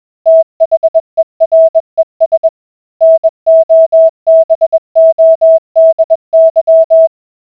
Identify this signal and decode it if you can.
Morse code, heard as a keyed tone, about 21 wpm, 650 Hz. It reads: THERES NOBODY